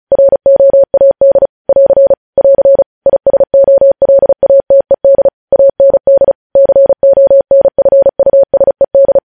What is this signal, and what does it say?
Morse code, heard as a keyed tone, about 36 wpm, 550 Hz. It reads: ROAD <AR> <AR> ISOLATED AND CONFUSED
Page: News headlines in Morse code at 35 wpm